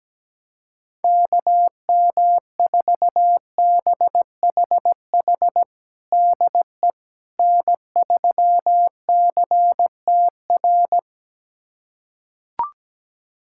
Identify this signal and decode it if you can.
Morse code, heard as a keyed tone, about 17 wpm, 700 Hz. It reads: KM4BHH DE N3CTR
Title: USA Callsign DE Callsign 17wpm